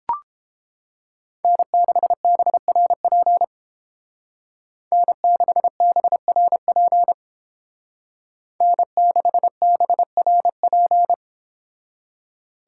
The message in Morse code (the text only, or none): N6BRP N6BRP N6BRP